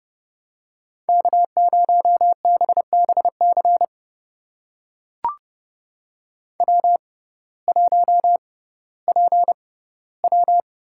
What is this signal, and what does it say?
Morse code, heard as a keyed tone, about 32 wpm, spaced as for 17 wpm, 700 Hz. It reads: K0BBC W1PW
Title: USA Call Signs - 5 Characters - 6x Character Spacing - 30wpm